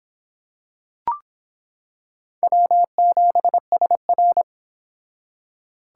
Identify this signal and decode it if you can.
Morse code, heard as a keyed tone, about 25 wpm, 700 Hz. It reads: W7SR